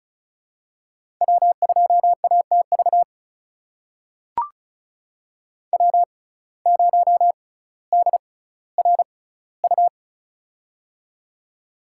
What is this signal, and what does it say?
Morse code, heard as a keyed tone, about 36 wpm, spaced as for 18 wpm, 700 Hz. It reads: W2ATV W0DRU
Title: USA Call Signs - 5 Characters - 6x Character Spacing - 35wpm